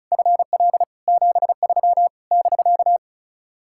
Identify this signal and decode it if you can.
Morse code, heard as a keyed tone, about 46 wpm, 700 Hz. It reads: FL 73 <BK>